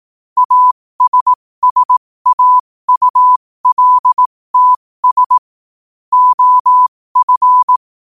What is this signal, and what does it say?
Morse code, heard as a keyed tone, about 18 wpm, spaced as for 15 wpm, 1000 Hz. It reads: ASSAULTS OF
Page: New quotes every day in morse code at 15 Words per minute.